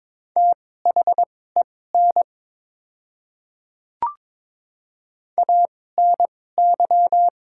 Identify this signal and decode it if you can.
Morse code, heard as a keyed tone, about 22 wpm, spaced as for 16 wpm, 700 Hz. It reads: THEN ANY
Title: 2x Character Spacing - Top 200 Words - 22wpm